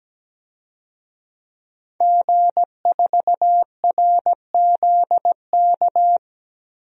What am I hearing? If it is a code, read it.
Morse code, heard as a keyed tone, about 17 wpm, 700 Hz. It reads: G4RZK
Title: DXCC-100 Call Signs - 5 Characters - Repeated 3x - 17wpm